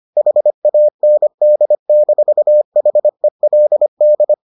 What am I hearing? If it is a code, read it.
Morse code, heard as a keyed tone, about 25 wpm, 600 Hz. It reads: HAND-HELD